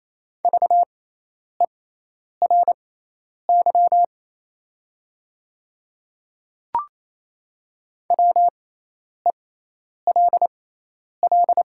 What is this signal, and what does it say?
Morse code, heard as a keyed tone, about 28 wpm, spaced as for 10 wpm, 700 Hz. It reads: VERY WELL
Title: 6x Character Spacing - Top 200 Words - 28wpm